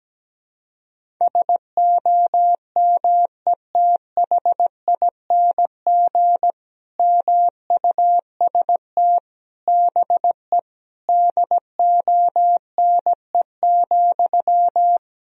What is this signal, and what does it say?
Morse code, heard as a keyed tone, about 17 wpm, 700 Hz. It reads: SOMETHING MUST BE DONE,
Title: Sentences from Top 1000 Words - Review Part 1 - 17wpm